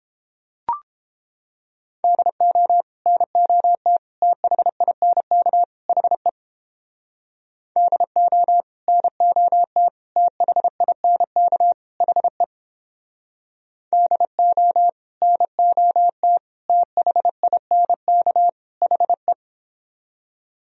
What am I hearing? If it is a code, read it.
Morse code, heard as a keyed tone, about 29 wpm, 700 Hz. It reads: DO NOT THINK HE DO NOT THINK HE DO NOT THINK HE